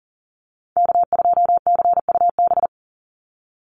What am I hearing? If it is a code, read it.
Morse code, heard as a keyed tone, about 40 wpm, 700 Hz. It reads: K2CUB
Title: Call Signs Repeated 3x - 40wpm